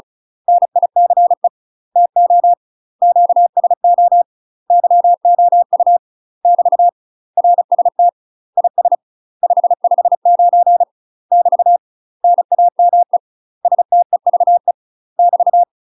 Morse code, heard as a keyed tone, about 34 wpm, 700 Hz. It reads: NICE TO QSO YOU <BT> RST IS 559 <BT> NAME STEVE <BT>
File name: UK MASTER QSOs TRAINING / 35WPM / UK MASTER QSOs 0002